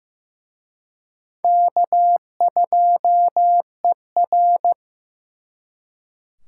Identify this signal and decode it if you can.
Morse code, heard as a keyed tone, about 18 wpm, 700 Hz. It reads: K2ER
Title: Speed Racing - USA Call Signs 15wpm